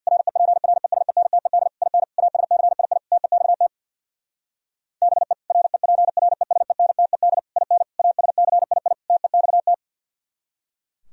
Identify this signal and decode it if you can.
Morse code, heard as a keyed tone, about 64 wpm, 700 Hz. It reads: REPRESENTED IN ASCII TEXT BE REPRESENTED IN ASCII TEXT